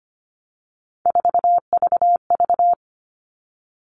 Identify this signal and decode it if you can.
Morse code, heard as a keyed tone, about 25 wpm, 700 Hz. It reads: 4VV
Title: All Cut Numbers - 3-Digit 25wpm